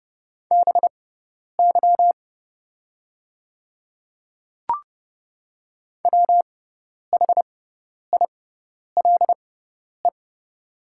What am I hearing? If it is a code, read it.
Morse code, heard as a keyed tone, about 30 wpm, spaced as for 10 wpm, 700 Hz. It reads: BY WHILE
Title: 6x Character Spacing - Top 300 Words - 30wpm